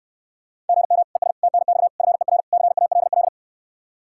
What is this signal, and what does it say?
Morse code, heard as a keyed tone, about 69 wpm, 700 Hz. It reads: GM ES TNX FER CALL